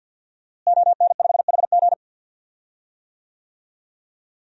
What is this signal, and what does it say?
Morse code, heard as a keyed tone, about 49 wpm, 700 Hz. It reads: ON5HG
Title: DXCC-100 Call Signs - 5 Characters - Repeated 3x - 50wpm